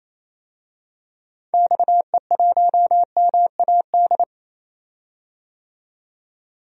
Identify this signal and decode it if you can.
Morse code, heard as a keyed tone, about 28 wpm, 700 Hz. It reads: XE1MAD
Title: DXCC-100 Call Signs - 6 Characters - 28wpm